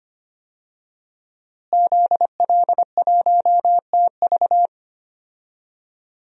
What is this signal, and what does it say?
Morse code, heard as a keyed tone, about 25 wpm, 700 Hz. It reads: ZL1TV
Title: DXCC-100 Call Signs - 5 Characters - Repeated 3x - 25wpm